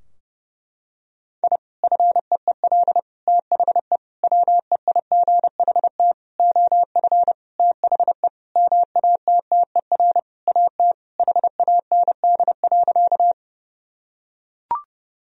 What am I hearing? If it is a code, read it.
Morse code, heard as a keyed tone, about 30 wpm, 700 Hz. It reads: I FEEL THE WEIGHT OF THE MATTER AT HAND.
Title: Sentences from Top 1000 Words - Review Part 3 - 30wpm